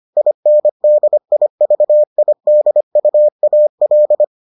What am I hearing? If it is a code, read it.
Morse code, heard as a keyed tone, about 25 wpm, 600 Hz. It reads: INDIVIDUAL